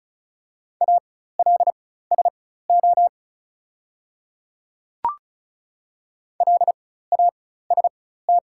Morse code, heard as a keyed tone, about 35 wpm, spaced as for 16 wpm, 700 Hz. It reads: ALSO LAST
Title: 4x Character Spacing - Top 300 Words - 35wpm